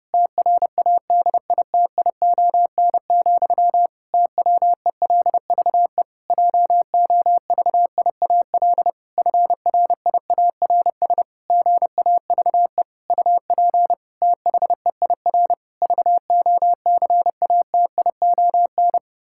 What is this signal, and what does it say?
Morse code, heard as a keyed tone, about 32 wpm, 700 Hz. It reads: TRADITION, TWELVE JOVIAL FRIARS GAVE UP THEIR VOCATION
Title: Pangram Challenge v2 - 30wpm